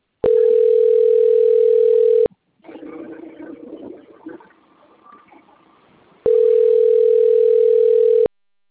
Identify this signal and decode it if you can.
Ringback tone (North America), 440 + 480 Hz.